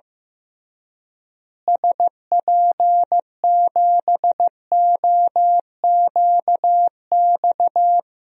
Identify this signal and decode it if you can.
Morse code, heard as a keyed tone, about 15 wpm, 700 Hz. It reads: SP7OQX
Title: DXCC-100 Call Signs - 6 Characters - Repeated 2x - 15wpm